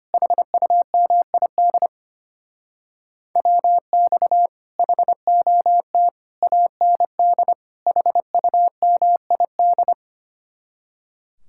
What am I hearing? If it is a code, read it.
Morse code, heard as a keyed tone, about 29 wpm, 700 Hz. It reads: HUMID WX HOT AND HUMID
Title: Speed-Racing Common QSO Phrases - 25wpm